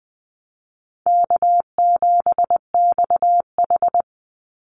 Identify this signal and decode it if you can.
Morse code, heard as a keyed tone, about 20 wpm, 700 Hz. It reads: K7XH
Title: Call Signs Repeated 2x - 20wpm